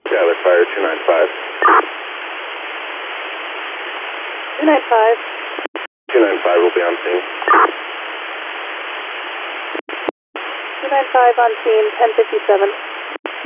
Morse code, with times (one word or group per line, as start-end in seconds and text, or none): none